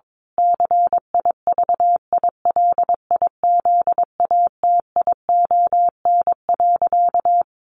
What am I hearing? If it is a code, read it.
Morse code, heard as a keyed tone, about 22 wpm, 700 Hz. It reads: CIVILIZATION.
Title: A Sound Of Thunder 22wpm